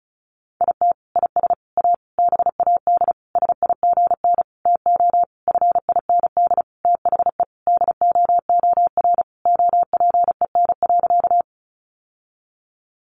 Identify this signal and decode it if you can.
Morse code, heard as a keyed tone, about 34 wpm, 700 Hz. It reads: IT IS A BAD SIGN TO FIND THE DOOR OPEN.
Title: Sentences from Top 1000 Words 35wpm